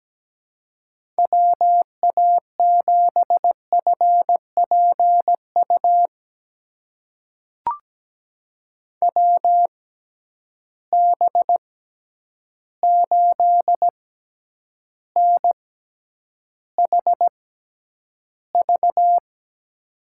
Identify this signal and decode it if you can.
Morse code, heard as a keyed tone, about 17 wpm, spaced as for 9 wpm, 700 Hz. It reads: WA7FPU WB8NHV
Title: USA Call Signs - 6 Characters - 6x Character Spacing - 17wpm